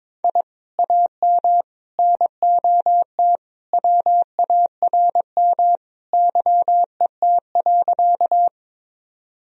Morse code, heard as a keyed tone, about 22 wpm, 700 Hz. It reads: I AM NOT WARM YET.
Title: Sentences from Top 1000 Words - Review Part 2 - 22wpm